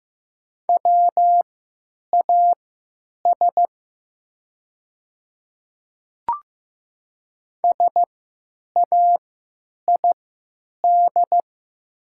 Morse code, heard as a keyed tone, about 15 wpm, spaced as for 9 wpm, 700 Hz. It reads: WAS SAID
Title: 3x Character Spacing - Top 500 Words - 15wpm